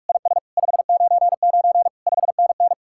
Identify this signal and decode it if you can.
Morse code, heard as a keyed tone, about 53 wpm, 700 Hz. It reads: IS 599 5NN
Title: Speed-Racing Common QSO Phrases Encore - 45wpm